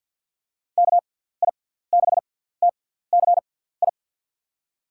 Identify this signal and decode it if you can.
Morse code, heard as a keyed tone, about 50 wpm, spaced as for 17 wpm, 700 Hz. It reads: KI6TCI
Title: USA Call Signs - 6 Characters - 6x Character Spacing - 50wpm